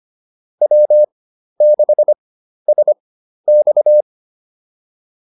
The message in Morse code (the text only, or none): W6SX